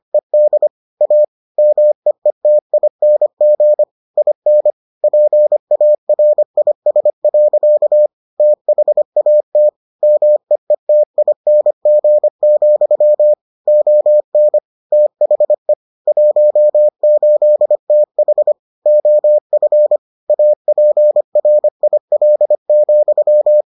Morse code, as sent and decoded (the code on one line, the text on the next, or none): . -.. / .- / -- . - .. -. --. / .. -. / .--. .- .-. .. ... .-.-.- / - .... .- - / -- . - .. -. --. --..-- / --- -. / - .... / .---- ---.. - .... / --- ..-. / .- .--. .-. .. .-.. --..--
ED A MEETING IN PARIS. THAT MEETING, ON THE 18TH OF APRIL,